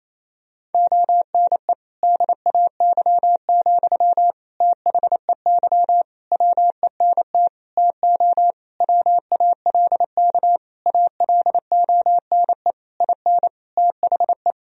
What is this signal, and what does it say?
Morse code, heard as a keyed tone, about 28 wpm, 700 Hz. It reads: ONE DAY, THEY WENT TO WALK ALONE IN THE
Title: Sentences from Top 1000 Words - Review Part 2 - 28wpm